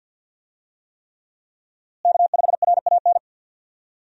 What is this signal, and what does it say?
Morse code, heard as a keyed tone, about 52 wpm, 700 Hz. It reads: K5RAN
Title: USA Call Signs - 5 Characters - Repeated 3x - 50wpm